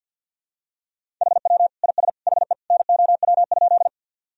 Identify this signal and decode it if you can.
Morse code, heard as a keyed tone, about 53 wpm, 700 Hz. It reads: HW IS HE NOW?
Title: Sentences from Top 100 Words and Abbreviations 50wpm